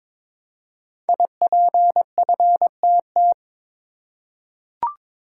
Morse code, heard as a keyed tone, about 23 wpm, 700 Hz. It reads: IPFTT
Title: Code Groups - 5 Letters - 22wpm